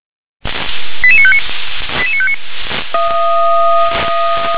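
Five-tone selective call (ZVEI), heard as quick runs of five tones, tones of about 70 ms. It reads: 88959 88959
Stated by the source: ZVEI five-tone selective call 88841, received on SDR.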